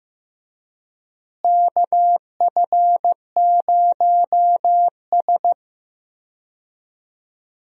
Morse code, heard as a keyed tone, about 15 wpm, 700 Hz. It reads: KF0S
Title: USA Call Signs - 4 Characters - Repeated 3x - 15wpm